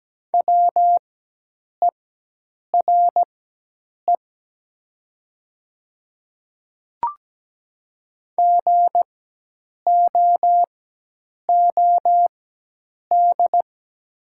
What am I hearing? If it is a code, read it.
Morse code, heard as a keyed tone, about 17 wpm, spaced as for 8 wpm, 700 Hz. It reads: WERE GOOD
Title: 4x Character Spacing - Top 200 Words - 17wpm